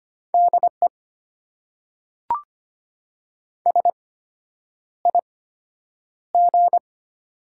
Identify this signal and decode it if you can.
Morse code, heard as a keyed tone, about 25 wpm, spaced as for 9 wpm, 700 Hz. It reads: DE SIG
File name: Top 90 QSO Elements - 8x Character Spacing - 25wpm